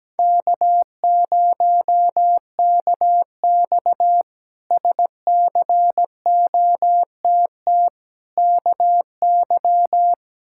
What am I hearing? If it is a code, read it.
Morse code, heard as a keyed tone, about 17 wpm, 700 Hz. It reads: K0KX SCOTT KY
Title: K1USN SST QSO Practice - 17wpm